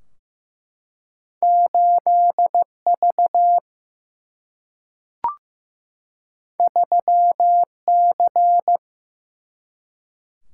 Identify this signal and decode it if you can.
Morse code, heard as a keyed tone, about 15 wpm, 700 Hz. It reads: 8V 3C
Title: Single-Letter Single-Number 15wpm